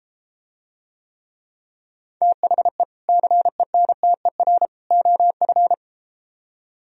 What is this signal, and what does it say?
Morse code, heard as a keyed tone, about 33 wpm, 700 Hz. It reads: THE CENTER OF